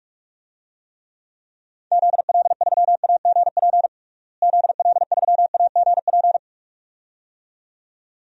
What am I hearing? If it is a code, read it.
Morse code, heard as a keyed tone, about 41 wpm, 700 Hz. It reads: ZL3AGP ZL3AGP
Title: DXCC-100 Call Signs - 6 Characters - Repeated 2x - 45wpm